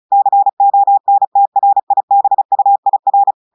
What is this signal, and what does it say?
Morse code, heard as a keyed tone, about 36 wpm, 800 Hz. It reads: CONTRIBUIR
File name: audios / portugais / portugais texte en clair 1 vitesse 35 mots minutes